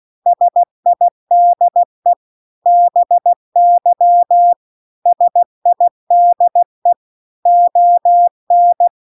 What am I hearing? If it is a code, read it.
Morse code, heard as a keyed tone, about 16 wpm, 700 Hz. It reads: SIDE BY SIDE ON